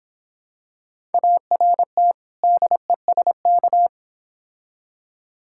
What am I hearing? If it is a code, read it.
Morse code, heard as a keyed tone, about 26 wpm, 700 Hz. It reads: ART DESK